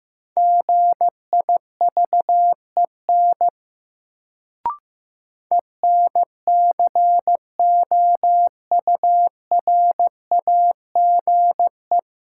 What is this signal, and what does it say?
Morse code, heard as a keyed tone, about 15 wpm, 700 Hz. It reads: GIVEN ENCOURAGE
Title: Sentences from Top 1400 Words - New Word Drill - 15wpm